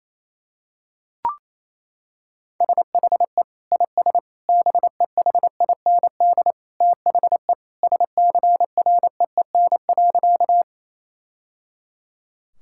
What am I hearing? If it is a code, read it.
Morse code, heard as a keyed tone, about 28 wpm, 700 Hz. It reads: SHE IS BEHIND THE SCREEN.
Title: Sentences from Top 1000 Words - Review Part 6 - 28wpm